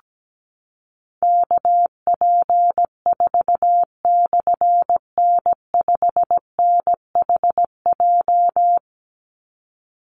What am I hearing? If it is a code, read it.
Morse code, heard as a keyed tone, about 17 wpm, 700 Hz. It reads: KP4/N5NHJ
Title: Call Signs Repeated 3x - 17wpm